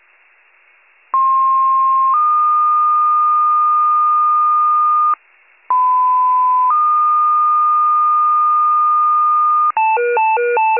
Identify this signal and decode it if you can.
Two-tone page, heard as a one-second tone, then a three-second tone, about 1060 then 1220 Hz, about 1010 then 1220 Hz. It is heard from 1.1 to 9.7 s.
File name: pager / 94 RESCUE OFC / wcalert